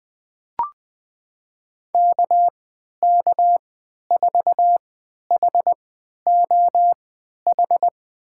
Morse code, heard as a keyed tone, about 20 wpm, spaced as for 11 wpm, 700 Hz. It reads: KK4HOH
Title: USA Call Signs - 6 Characters - 3x Character Spacing - 20wpm